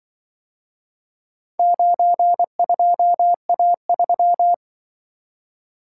Morse code, heard as a keyed tone, about 24 wpm, 700 Hz. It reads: 92A3
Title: Speed-Racing - Four Letter-Number Combinations - 20wpm